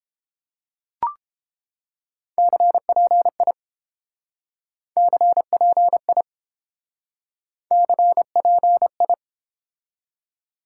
Morse code, heard as a keyed tone, about 28 wpm, 700 Hz. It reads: CPI CPI CPI